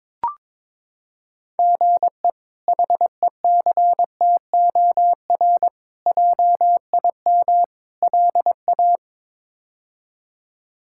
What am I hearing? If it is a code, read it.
Morse code, heard as a keyed tone, about 22 wpm, 700 Hz. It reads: GE HECTOR JIM LA
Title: K1USN SST QSO Practice - 22wpm